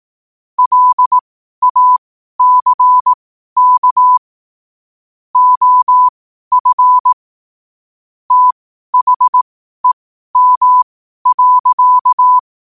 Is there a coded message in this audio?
Yes: Morse code from 0.6 to 12.4 s, heard as a keyed tone, about 18 wpm, spaced as for 12 wpm, 1000 Hz. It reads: LACK OF THEM.